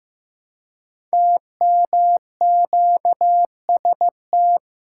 Morse code, heard as a keyed tone, about 15 wpm, 700 Hz. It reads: TMQST
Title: Code Groups - 5 Letter:Numbers - 15wpm